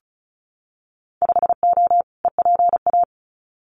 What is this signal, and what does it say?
Morse code, heard as a keyed tone, about 36 wpm, 700 Hz. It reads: 5O EPA
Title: Winter Field Day Exchanges Encore 35wpm